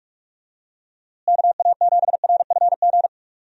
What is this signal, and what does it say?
Morse code, heard as a keyed tone, about 43 wpm, 700 Hz. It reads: KA7RFG
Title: USA Call Signs - 6 Characters - Repeated 2x - 45wpm